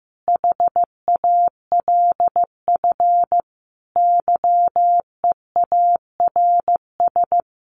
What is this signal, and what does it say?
Morse code, heard as a keyed tone, about 15 wpm, 700 Hz. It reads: HALF YEARS
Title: Sets of 5 Words Encore2 15wpm